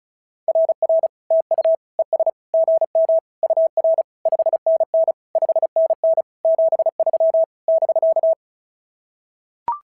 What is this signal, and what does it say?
Morse code, heard as a keyed tone, about 36 wpm, 650 Hz. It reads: RR TU ES GM UR 5NN 5NN 73 <BK>
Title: SOTA QSO Practice 35wpm